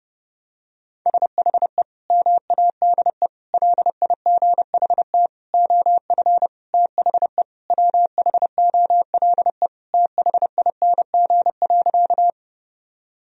Morse code, heard as a keyed tone, about 32 wpm, 700 Hz. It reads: SHE MADE LIGHT OF THE WHOLE THING.
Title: Sentences from Top 1000 Words - Review Part 1 - 30wpm